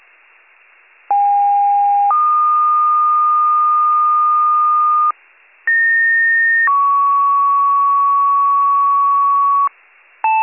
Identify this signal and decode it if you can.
Two-tone page, heard as a one-second tone, then a three-second tone, about 810 then 1220 Hz, about 1850 then 1120 Hz. It is heard from 1.1 to 9.7 s.